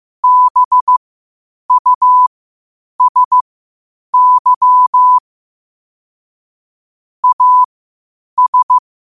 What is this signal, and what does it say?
Morse code, heard as a keyed tone, about 15 wpm, spaced as for 8 wpm, 1000 Hz. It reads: BUSY AS
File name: qotdfiles / 8wpm